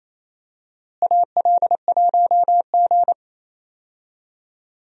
Morse code, heard as a keyed tone, about 28 wpm, 700 Hz. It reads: AL1G
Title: USA Call Signs - 4 Characters - Repeated 2x - 28wpm